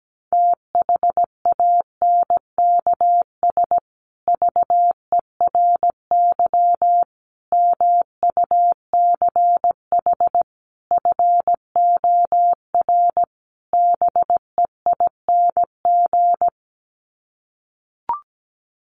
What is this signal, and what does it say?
Morse code, heard as a keyed tone, about 17 wpm, 700 Hz. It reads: THANKS VERY MUCH FOR BEING
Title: Sets of 5 Words Encore 17wpm